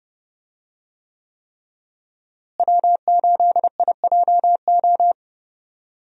Morse code, heard as a keyed tone, about 30 wpm, 700 Hz. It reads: W8IJO
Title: USA Call Signs - 5 Characters - Repeated 3x - 30wpm